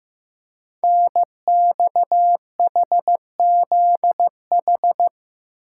Morse code, heard as a keyed tone, about 15 wpm, 700 Hz. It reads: NXHZH
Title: Code Groups - 5 Letters - 15wpm